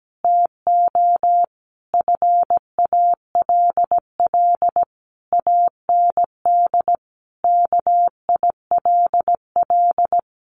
Morse code, heard as a keyed tone, about 17 wpm, 700 Hz. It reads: TO FALL AND KILL
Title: A Sound Of Thunder 17wpm